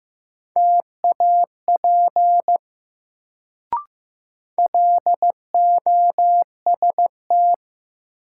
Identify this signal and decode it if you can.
Morse code, heard as a keyed tone, about 15 wpm, 700 Hz. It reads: TAP LOST
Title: Sentences from Top 1400 Words - New Word Drill - 15wpm